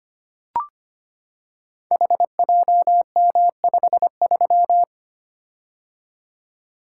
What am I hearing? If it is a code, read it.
Morse code, heard as a keyed tone, about 25 wpm, 700 Hz. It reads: HJM53
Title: Code Groups - 5 Letter:Numbers - 25wpm